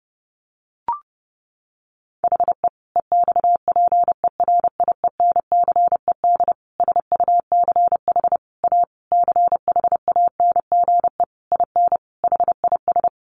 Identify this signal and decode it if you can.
Morse code, heard as a keyed tone, about 30 wpm, 700 Hz. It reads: HE EXPERIENCED SUCH A CHANGE IN HIS
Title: Sentences from Top 1100 Words 30wpm